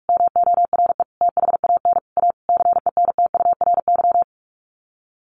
Morse code, heard as a keyed tone, about 44 wpm, 700 Hz. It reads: MORE THAN A CENTURY
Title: Sets of 5 Words Encore2 45wpm